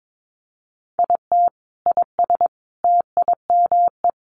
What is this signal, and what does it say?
Morse code, heard as a keyed tone, about 22 wpm, 700 Hz. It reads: IT IS TIME
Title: Sentences from Top 500 Words 22wpm